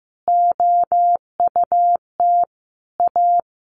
Morse code, heard as a keyed tone, about 15 wpm, 700 Hz. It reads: OUT A
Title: Sentences from Top 1500 Words 15wpm